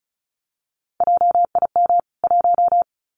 Morse code, heard as a keyed tone, about 34 wpm, 700 Hz. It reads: JIM 1
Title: CWT Exchange 35wpm